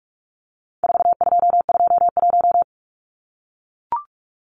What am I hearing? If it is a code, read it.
Morse code, heard as a keyed tone, about 44 wpm, 700 Hz. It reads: 4221
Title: Numbers - 4-Digit 45wpm